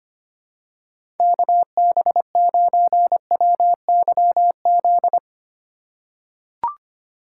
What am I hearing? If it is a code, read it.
Morse code, heard as a keyed tone, about 25 wpm, 700 Hz. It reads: KB9WYZ
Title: USA Call Signs - 6 Characters - 2x Character Spacing - 25wpm